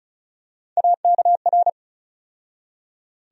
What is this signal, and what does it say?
Morse code, heard as a keyed tone, about 34 wpm, 700 Hz. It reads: AKR
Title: USA Call Sign Postfixes - 3 Letters 35wpm